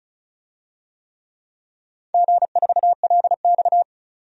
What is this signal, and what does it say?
Morse code, heard as a keyed tone, about 34 wpm, 700 Hz. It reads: G4LX
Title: DXCC-100 Call Signs - 4 Characters - 35wpm